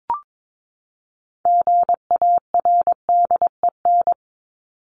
Morse code, heard as a keyed tone, about 22 wpm, 700 Hz. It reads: GARDEN
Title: Top 501-750 Words 22wpm